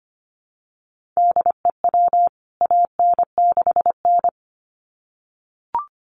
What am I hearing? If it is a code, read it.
Morse code, heard as a keyed tone, about 25 wpm, 700 Hz. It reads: DEW AN6N
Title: CWT Exchange 25wpm